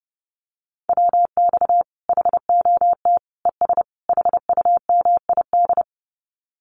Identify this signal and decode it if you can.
Morse code, heard as a keyed tone, about 30 wpm, 700 Hz. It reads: WX HOT ES HUMID
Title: Common QSO Phrases 30wpm